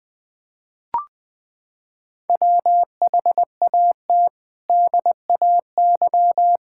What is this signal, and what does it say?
Morse code, heard as a keyed tone, about 20 wpm, 700 Hz. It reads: WHAT DAY
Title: Sentences from Top 1000 Words - Review Part 5 - 20wpm